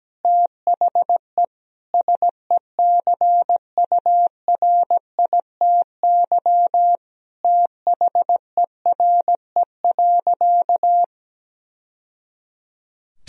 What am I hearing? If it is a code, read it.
Morse code, heard as a keyed tone, about 17 wpm, 700 Hz. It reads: THE SECURITY THERE.
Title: Sentences from Top 1000 Words - Review Part 4 - 17wpm